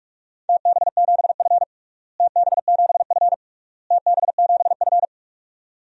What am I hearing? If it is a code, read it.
Morse code, heard as a keyed tone, about 46 wpm, 700 Hz. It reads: TB7F TB7F TB7F
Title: DXCC-100 Call Signs - 4 Characters - Repeated 3x - 45wpm